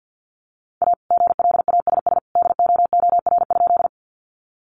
Morse code, heard as a keyed tone, about 50 wpm, 700 Hz. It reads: A GLASS DOOR?
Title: Sentences from Top 1300 Words 50wpm